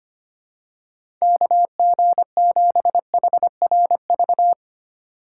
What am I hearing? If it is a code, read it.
Morse code, heard as a keyed tone, about 25 wpm, 700 Hz. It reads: KG7HRV
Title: USA Call Signs - 2x Character Spacing - 25wpm